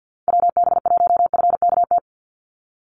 Morse code, heard as a keyed tone, about 50 wpm, 700 Hz. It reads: WB1FKT
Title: USA Call Signs 50wpm